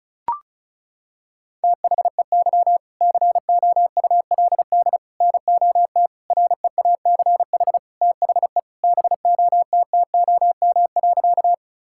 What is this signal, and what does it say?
Morse code, heard as a keyed tone, about 36 wpm, 700 Hz. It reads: THEY COULD NOT REACH THE BOTTOM.
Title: Sentences from Top 1000 Words - Review Part 6 - 35wpm